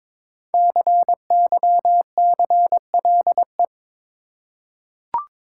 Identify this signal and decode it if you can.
Morse code, heard as a keyed tone, about 22 wpm, 700 Hz. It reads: CYCLE
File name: 5 Letter Words - First 3 Letters Given - 22wpm